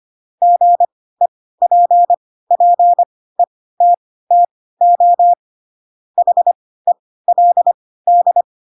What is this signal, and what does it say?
Morse code, heard as a keyed tone, about 25 wpm, spaced as for 16 wpm, 700 Hz. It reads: GEPPETTO HELD
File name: Pinocchio / Pinocchio ch3 10 25 ik1hge